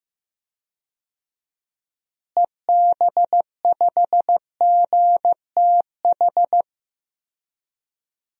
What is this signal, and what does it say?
Morse code, heard as a keyed tone, about 15 wpm, 700 Hz. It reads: EB5GTH
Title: DXCC-100 Call Signs - 6 Characters - Repeated 3x - 15wpm